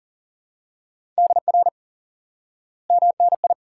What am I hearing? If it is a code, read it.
Morse code, heard as a keyed tone, about 40 wpm, 700 Hz. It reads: DR MNI
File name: IQE Course 3 - Lesson 89 - YAGI - Two Elements - 40wpm